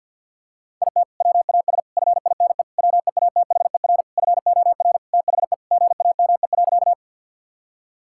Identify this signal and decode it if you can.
Morse code, heard as a keyed tone, about 48 wpm, 700 Hz. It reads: IT WAS FINE WEATHER FOR THE GAME.
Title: Sentences from Top 1000 Words - Review Part 5 - 50wpm